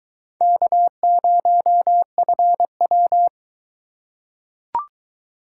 Morse code, heard as a keyed tone, about 21 wpm, 700 Hz. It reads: K0FW